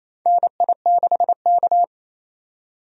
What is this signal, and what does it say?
Morse code, heard as a keyed tone, about 28 wpm, 700 Hz. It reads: NI6K
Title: USA Call Signs - 4 Characters - 28wpm